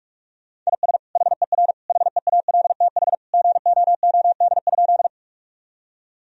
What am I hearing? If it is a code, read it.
Morse code, heard as a keyed tone, about 46 wpm, 700 Hz. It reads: IS HER HEALTH GOOD?
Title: Sentences from Top 1000 Words - Review Part 3 - 45wpm